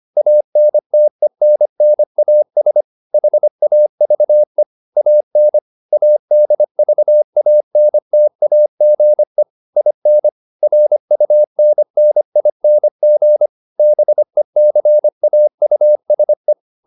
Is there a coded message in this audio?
Yes: Morse code from 0.2 to 16.5 s, heard as a keyed tone, about 25 wpm, 600 Hz. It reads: ANTENNAS HAVE AN ADVANTAGE IN RUNNING BECAUSE